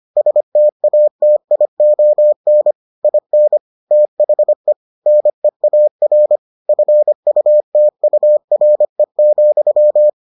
Morse code, heard as a keyed tone, about 25 wpm, 600 Hz. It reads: STATION IN THE NEAR FUTURE,